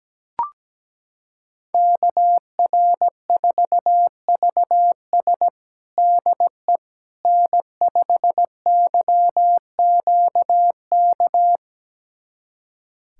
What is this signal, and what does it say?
Morse code, heard as a keyed tone, about 17 wpm, 700 Hz. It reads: KR4VS DE N5YQK
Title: USA Callsign DE Callsign - 5 Characters - 17wpm